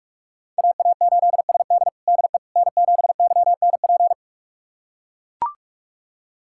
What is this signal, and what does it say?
Morse code, heard as a keyed tone, about 44 wpm, 700 Hz. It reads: AA8SD DE N7YNP
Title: USA Callsign DE Callsign - 5 Characters - 45wpm